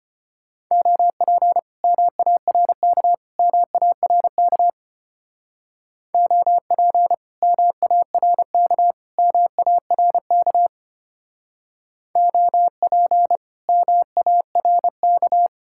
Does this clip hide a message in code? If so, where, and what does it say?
Yes: Morse code from 0.7 to 15.5 s, heard as a keyed tone, about 33 wpm, 700 Hz. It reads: OP MARK MARK OP MARK MARK OP MARK